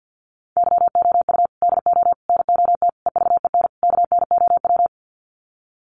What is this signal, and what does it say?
Morse code, heard as a keyed tone, about 50 wpm, 700 Hz. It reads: YOU DO NOT EVEN KNOW
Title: Sets of 5 Words Encore2 50wpm